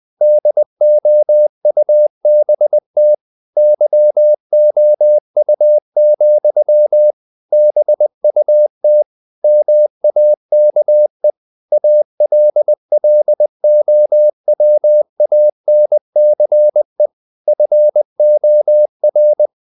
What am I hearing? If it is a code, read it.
Morse code, heard as a keyed tone, about 20 wpm, 600 Hz. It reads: DOUBT YOU, BUT MAKE ALLOWANCE FOR
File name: CW / Practice / If20